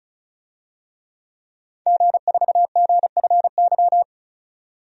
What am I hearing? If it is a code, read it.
Morse code, heard as a keyed tone, about 36 wpm, 700 Hz. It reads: G4GFY
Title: DXCC-100 Call Signs - 5 Characters - 35wpm